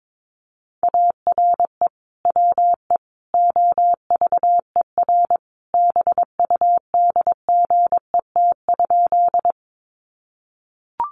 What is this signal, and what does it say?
Morse code, heard as a keyed tone, about 22 wpm, 700 Hz. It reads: ARE WE OVER BUDGET?
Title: Sentences from Top 1200 Words 22wpm